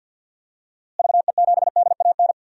Morse code, heard as a keyed tone, about 50 wpm, 700 Hz. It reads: VE7DAN
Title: DXCC-100 Call Signs - 6 Characters - Repeated 2x - 50wpm